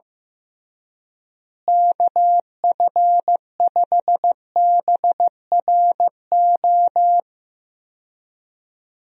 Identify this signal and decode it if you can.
Morse code, heard as a keyed tone, about 15 wpm, 700 Hz. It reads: KF5BRO
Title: USA Call Signs - 6 Characters - Repeated 3x - 15wpm